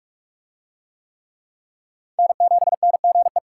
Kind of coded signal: Morse code

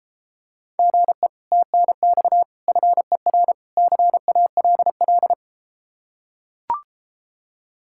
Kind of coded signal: Morse code

